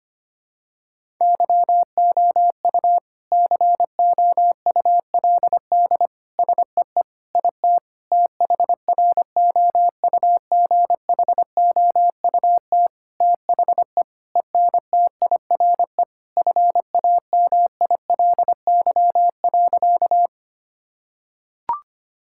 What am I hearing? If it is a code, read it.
Morse code, heard as a keyed tone, about 25 wpm, 700 Hz. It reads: YOU COULD SEE IT THROUGHOUT THE ENTIRE FAMILY.
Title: Sentences from Top 1000 Words - Review Part 5 - 25wpm